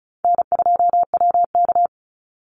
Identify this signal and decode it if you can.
Morse code, heard as a keyed tone, about 34 wpm, 700 Hz. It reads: N2WK
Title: Call Signs Repeated 2x - 35wpm